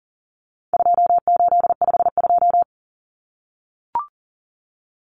Morse code, heard as a keyed tone, about 40 wpm, 700 Hz. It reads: 2852
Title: Common Cut Numbers - 4-Digit 40wpm